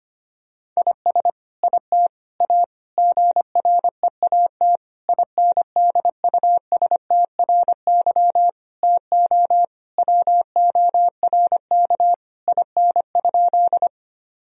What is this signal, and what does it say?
Morse code, heard as a keyed tone, about 25 wpm, 700 Hz. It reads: IS IT A GREAT INDUSTRY TO WORK IN?
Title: Sentences from Top 1000 Words - Review Part 4 - 25wpm